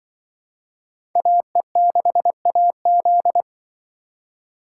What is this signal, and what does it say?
Morse code, heard as a keyed tone, about 23 wpm, 700 Hz. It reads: AE6AZ
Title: Speed Racing - USA 2x2 Call Signs 20wpm